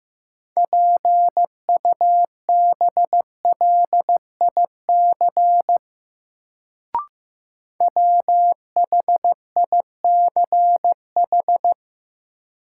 Morse code, heard as a keyed tone, about 15 wpm, 700 Hz. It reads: PUBLIC WHICH
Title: Sets of 2 Words - New Word Drill - 15wpm